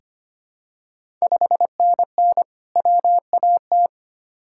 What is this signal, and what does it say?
Morse code, heard as a keyed tone, about 25 wpm, 700 Hz. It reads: 5NN WAT
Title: Ontario QSO Party Exchange 25wpm